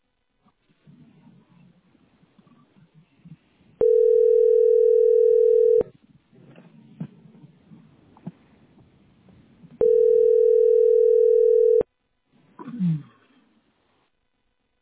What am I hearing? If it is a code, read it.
Ringback tone (North America), 440 + 480 Hz.